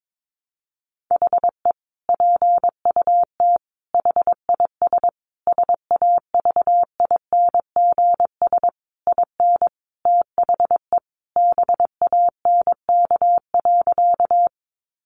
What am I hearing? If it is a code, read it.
Morse code, heard as a keyed tone, about 22 wpm, 700 Hz. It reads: HE PUT HIS SAVINGS IN THE BANK.
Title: Sentences from Top 1200 Words 22wpm